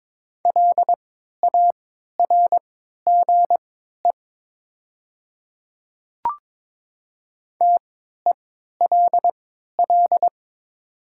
Morse code, heard as a keyed tone, about 22 wpm, spaced as for 12 wpm, 700 Hz. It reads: LARGE TELL
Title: 3x Character Spacing - Top 300 Words - 22wpm